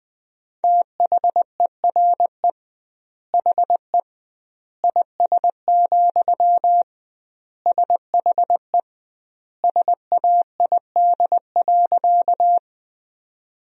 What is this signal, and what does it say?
Morse code, heard as a keyed tone, about 20 wpm, 700 Hz. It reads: THERE HE IS, SHE SAID.
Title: 2x Word-Spacing - Sentences from Top 100 Words - 20wpm